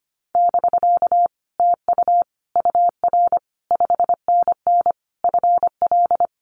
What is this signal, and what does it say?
Morse code, heard as a keyed tone, about 25 wpm, 700 Hz. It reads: <BK> TU UR 5NN FL
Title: Common QSO Phrases Encore 25wpm